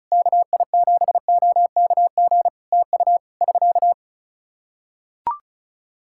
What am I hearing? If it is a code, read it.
Morse code, heard as a keyed tone, about 34 wpm, 700 Hz. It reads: KI7OKG TU <SK>
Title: K1USN SST QSO Practice 35wpm